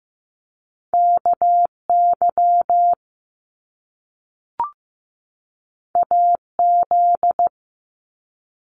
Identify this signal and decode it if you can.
Morse code, heard as a keyed tone, about 15 wpm, 700 Hz. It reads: KY AZ
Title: US States 15wpm